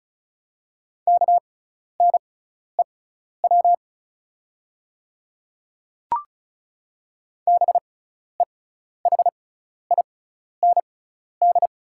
Morse code, heard as a keyed tone, about 34 wpm, spaced as for 12 wpm, 690 Hz. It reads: KNEW BEHIND
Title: 6x Character Spacing - Top 500 Words - 35wpm